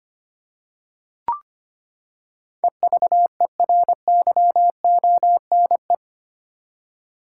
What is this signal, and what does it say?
Morse code, heard as a keyed tone, about 25 wpm, 700 Hz. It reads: EVERYONE
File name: Eight Letter Words - 25wpm